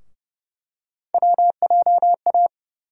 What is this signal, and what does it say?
Morse code, heard as a keyed tone, about 30 wpm, 700 Hz. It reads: WJA
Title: USA Call Sign Postfixes - 3 Letters 30wpm